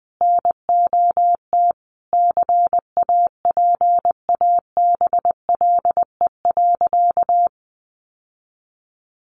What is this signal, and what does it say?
Morse code, heard as a keyed tone, about 20 wpm, 700 Hz. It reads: NOT CAPABLE.
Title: Sentences from Top 1200 Words 20wpm